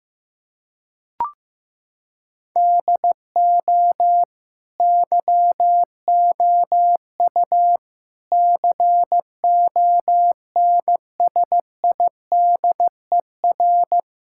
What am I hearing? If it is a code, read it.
Morse code, heard as a keyed tone, about 15 wpm, 700 Hz. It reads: DO YOU CONSIDER
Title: Sentences from Top 1000 Words - Review Part 6 - 15wpm